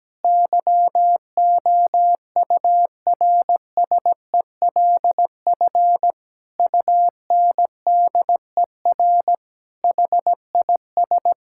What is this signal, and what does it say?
Morse code, heard as a keyed tone, about 17 wpm, 700 Hz. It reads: YOURSELF UNDER HIS
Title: Sentences from Top 1600 Words 17wpm